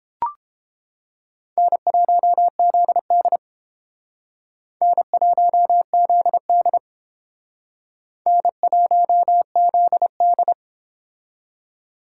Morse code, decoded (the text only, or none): N1ZD N1ZD N1ZD